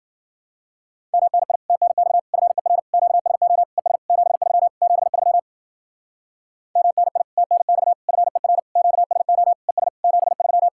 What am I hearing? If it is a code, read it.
Morse code, heard as a keyed tone, about 51 wpm, 700 Hz. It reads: MNI TNX FER QSO ES 73 73 MNI TNX FER QSO ES 73